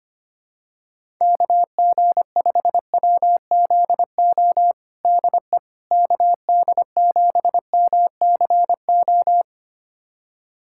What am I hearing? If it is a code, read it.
Morse code, heard as a keyed tone, about 25 wpm, 700 Hz. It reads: KG5WZO DE KD7MCO
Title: USA Callsign DE Callsign 25wpm